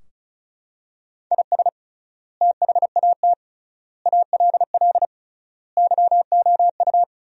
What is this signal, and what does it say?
Morse code, heard as a keyed tone, about 36 wpm, 700 Hz. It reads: IS THAT ALL YOU
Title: 3x Word-Spacing - Sentences from Top 200 Words - 35wpm